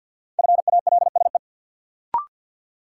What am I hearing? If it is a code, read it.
Morse code, heard as a keyed tone, about 52 wpm, 700 Hz. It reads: FALSE
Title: Sentences from Top 1600 Words - New Word Drill Alt - 50wpm